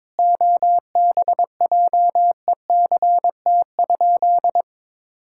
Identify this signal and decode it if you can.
Morse code, heard as a keyed tone, about 22 wpm, 700 Hz. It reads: OBJECT?
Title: Sentences from Top 1000 Words - Review Part 5 - 22wpm